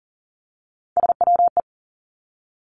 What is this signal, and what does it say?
Morse code, heard as a keyed tone, about 40 wpm, 700 Hz. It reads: SWE
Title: Three Challenging Letter-Number Combinations 40wpm